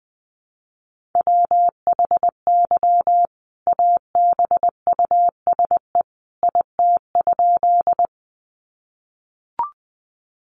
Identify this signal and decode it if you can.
Morse code, heard as a keyed tone, about 20 wpm, 700 Hz. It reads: WHY ABUSE IT?
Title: Sentences from Top 1300 Words 20wpm